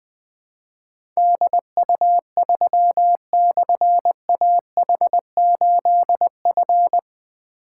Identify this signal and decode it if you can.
Morse code, heard as a keyed tone, about 20 wpm, 700 Hz. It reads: DU3/AH8F
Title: Difficult Call Signs 20wpm